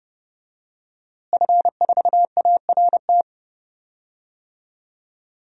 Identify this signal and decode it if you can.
Morse code, heard as a keyed tone, about 30 wpm, 700 Hz. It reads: F4ART
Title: DXCC-100 Call Signs - 5 Characters - Repeated 2x - 30wpm